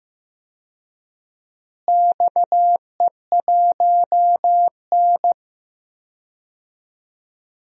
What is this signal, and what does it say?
Morse code, heard as a keyed tone, about 15 wpm, 700 Hz. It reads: XE1N
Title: DXCC-100 Call Signs - 4 Characters - Repeated 2x - 15wpm